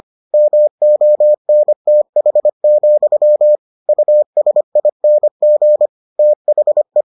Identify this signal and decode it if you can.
Morse code, heard as a keyed tone, about 25 wpm, 600 Hz. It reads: MONTH, USING THE